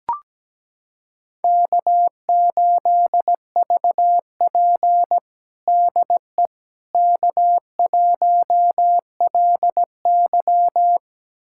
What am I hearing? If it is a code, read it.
Morse code, heard as a keyed tone, about 17 wpm, 700 Hz. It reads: K8VP DE K1LY
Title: USA Callsign DE Callsign - 4 Characters - 17wpm